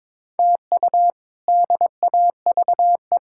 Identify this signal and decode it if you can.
Morse code, heard as a keyed tone, about 22 wpm, 700 Hz. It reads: TU DAVE
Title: K1USN SST QSO Practice - 22wpm